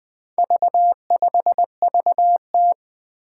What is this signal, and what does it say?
Morse code, heard as a keyed tone, about 20 wpm, 700 Hz. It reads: V5VT
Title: Four Challenging Letter-Number Combinations 20wpm